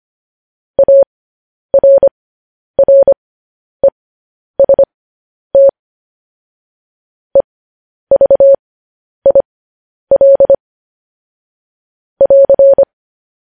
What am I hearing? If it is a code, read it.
Morse code, heard as a keyed tone, about 25 wpm, spaced as for 10 wpm, 550 Hz. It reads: ARREST EVIL <AR>
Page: news-headlines-in-morse-code-at-10-wpm